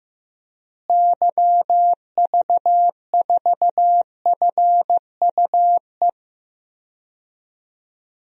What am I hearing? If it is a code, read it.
Morse code, heard as a keyed tone, about 15 wpm, 700 Hz. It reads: YV4FUE
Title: DXCC-100 Call Signs - 6 Characters - Repeated 2x - 15wpm